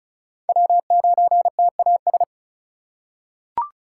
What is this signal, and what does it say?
Morse code, heard as a keyed tone, about 36 wpm, 700 Hz. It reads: W9TAS
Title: USA Call Signs - 5 Characters - 4x Character Spacing - 35wpm